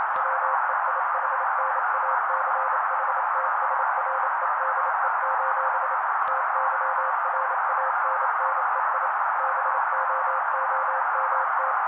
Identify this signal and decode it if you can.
Morse code, heard as a keyed tone, about 28 wpm, 530 Hz. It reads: WEIHNACHTSREDE: TYRANNEI BOOMT
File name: daily / 27wpm noise